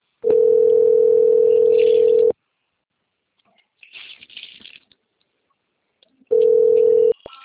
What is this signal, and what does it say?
Ringback tone (North America), 440 + 480 Hz.